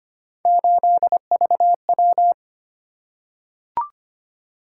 Morse code, heard as a keyed tone, about 25 wpm, 700 Hz. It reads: 8VW
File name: Three Letter-Number Combinations - 25wpm